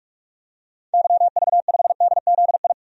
Morse code, heard as a keyed tone, about 46 wpm, 700 Hz. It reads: YV5DZI
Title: DXCC-100 Call Signs - 6 Characters - 45wpm